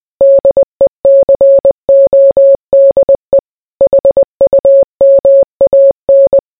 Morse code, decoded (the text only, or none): DECODE HUMAN